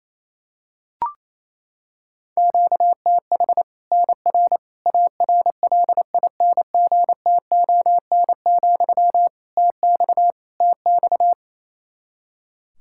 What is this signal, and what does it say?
Morse code, heard as a keyed tone, about 28 wpm, 700 Hz. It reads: QTH NR ARLINGTON, TX TX
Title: Practice QSO - Exchange 1 - Beginning - 28wpm